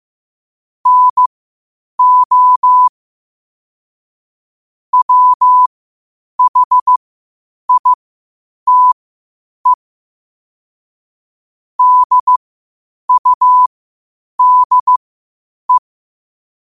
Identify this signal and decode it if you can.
Morse code, heard as a keyed tone, about 15 wpm, spaced as for 8 wpm, 1000 Hz. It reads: NO WHITE DUDE
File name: qotdfiles / 8wpm